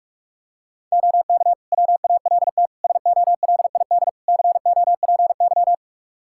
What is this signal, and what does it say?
Morse code, heard as a keyed tone, about 46 wpm, 700 Hz. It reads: OK WALT SOLID COPY
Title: Practice QSO - Exchange 3 - More Info - 45wpm